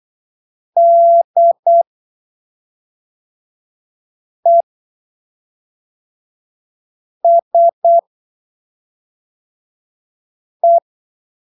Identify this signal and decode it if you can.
Morse code, heard as a keyed tone, about 8 wpm, spaced as for 3 wpm, 680 Hz. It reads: DESE